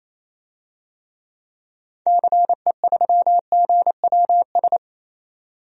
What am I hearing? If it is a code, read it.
Morse code, heard as a keyed tone, about 28 wpm, 700 Hz. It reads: CE3GWS
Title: DXCC-100 Call Signs - 6 Characters - 28wpm